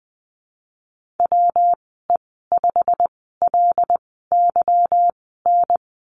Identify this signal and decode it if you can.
Morse code, heard as a keyed tone, about 19 wpm, spaced as for 14 wpm, 700 Hz. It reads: WE5LYN
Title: 193 Lesson 31g - Letter v - Call Signs 20wpm